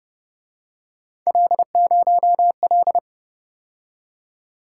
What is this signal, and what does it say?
Morse code, heard as a keyed tone, about 30 wpm, 700 Hz. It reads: L0L
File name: Speed-Racing - Three Letter-Number Combinations - 25wpm